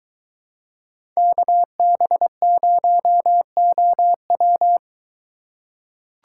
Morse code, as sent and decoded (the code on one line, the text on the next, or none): -.- -... ----- --- .--
KB0OW